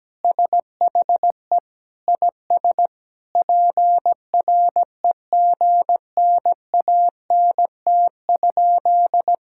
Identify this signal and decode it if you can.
Morse code, heard as a keyed tone, about 17 wpm, 700 Hz. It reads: SHE IS PREGNANT?
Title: Sentences from Top 1000 Words - Review Part 6 - 17wpm